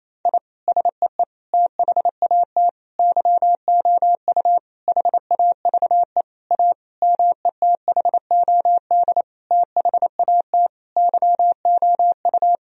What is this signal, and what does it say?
Morse code, heard as a keyed tone, about 28 wpm, 700 Hz. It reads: I SEE THAT YOU HAVE A METHOD THAT YOU
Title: Sentences from Top 1000 Words - Review Part 3 - 28wpm